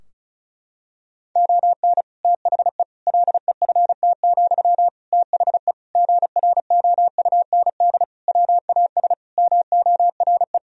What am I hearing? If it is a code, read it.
Morse code, heard as a keyed tone, about 34 wpm, 700 Hz. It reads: ON THE LEFT, THE GROUND WAS MORE
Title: Sentences from Top 1000 Words - Review Part 4 - 35wpm